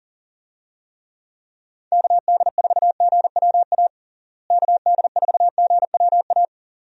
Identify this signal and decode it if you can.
Morse code, heard as a keyed tone, about 37 wpm, 700 Hz. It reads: KD4GWA KD4GWA
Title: USA Call Signs - 6 Characters - Repeated 2x - 40wpm